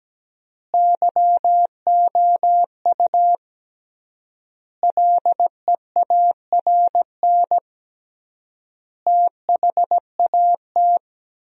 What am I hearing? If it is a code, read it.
Morse code, heard as a keyed tone, about 17 wpm, 700 Hz. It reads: YOU LEARN THAT
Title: 3x Word-Spacing - Sentences from Top 300 Words - 17wpm